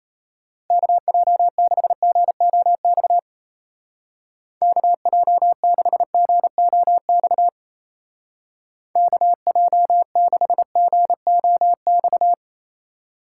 Morse code, heard as a keyed tone, about 34 wpm, 700 Hz. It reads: KJ6GOX KJ6GOX KJ6GOX
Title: Speed Racing - USA 2x3 Call Signs 28wpm